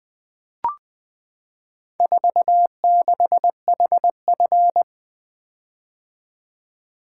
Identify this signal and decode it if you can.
Morse code, heard as a keyed tone, about 20 wpm, 700 Hz. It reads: 46HF
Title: Four Letter-Number Combinations - 20wpm